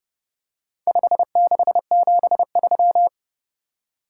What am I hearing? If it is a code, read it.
Morse code, heard as a keyed tone, about 30 wpm, 700 Hz. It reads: 5673